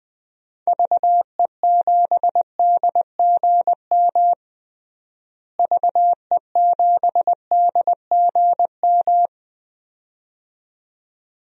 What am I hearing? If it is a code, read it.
Morse code, heard as a keyed tone, about 21 wpm, 700 Hz. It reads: VE7DGM VE7DGM
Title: DXCC-100 Call Signs - 6 Characters - Repeated 2x - 20wpm